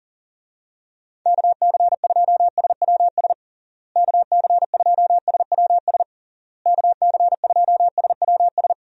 Morse code, heard as a keyed tone, about 40 wpm, 700 Hz. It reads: KC2SWS KC2SWS KC2SWS
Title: USA Call Signs - 6 Characters - Repeated 3x - 40wpm